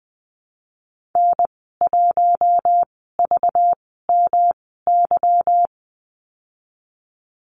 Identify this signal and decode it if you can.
Morse code, heard as a keyed tone, about 20 wpm, spaced as for 15 wpm, 700 Hz. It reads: N1VMY
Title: 200 Lesson 32g - Letter k - Call Signs 20wpm